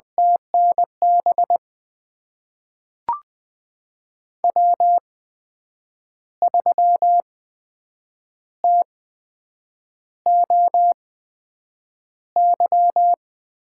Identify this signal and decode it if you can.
Morse code, heard as a keyed tone, about 20 wpm, spaced as for 7 wpm, 700 Hz. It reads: TNB W3TOY
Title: USA Call Signs - 8x Character Spacing - 20wpm